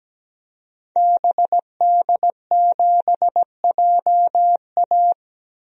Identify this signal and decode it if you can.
Morse code, heard as a keyed tone, about 17 wpm, 700 Hz. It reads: BD7JA
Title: DXCC-100 Call Signs - 5 Characters - 17wpm